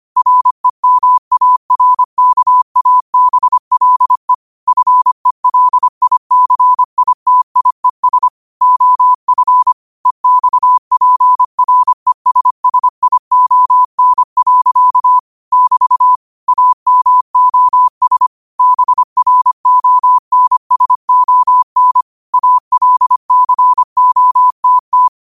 Morse code, heard as a keyed tone, about 25 wpm, 1000 Hz. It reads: REMARKABLE FELICITIES OF EXPRESSION. <BT> AMOS BRONSON ALCOTT
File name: qotdfiles / 25wpm